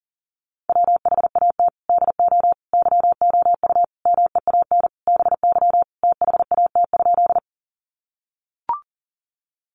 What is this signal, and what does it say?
Morse code, heard as a keyed tone, about 40 wpm, 700 Hz. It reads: WHAT DO YOU MEAN BY THAT?
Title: Sentences from Top 200 Words 40wpm